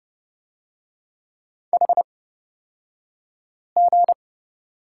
Morse code, heard as a keyed tone, about 30 wpm, spaced as for 5 wpm, 700 Hz. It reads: HG
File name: Lesson 37d - Letter x - Two Characters 30wpm